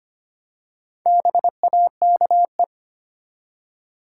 Morse code, heard as a keyed tone, about 25 wpm, 700 Hz. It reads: BAKE
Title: Four Letter Words - 25wpm